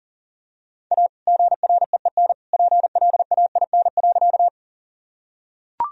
Morse code, heard as a keyed tone, about 40 wpm, 700 Hz. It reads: A GREEN PLAIN.
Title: Sentences from Top 1000 Words - Review Part 2 - 40wpm